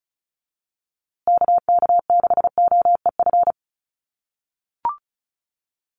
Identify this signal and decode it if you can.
Morse code, heard as a keyed tone, about 34 wpm, 700 Hz. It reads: KK6OEF
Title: USA Call Signs Repeated 3x 35wpm